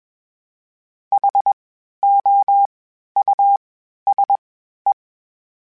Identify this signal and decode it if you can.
Morse code, heard as a keyed tone, about 21 wpm, 800 Hz. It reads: H O U S E
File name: General / General-mp3s / Houses @20